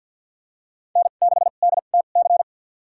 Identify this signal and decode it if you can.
Morse code, heard as a keyed tone, about 53 wpm, spaced as for 37 wpm, 700 Hz. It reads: N6DTC
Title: USA Call Signs - 5 Characters - 2x Character Spacing - 50wpm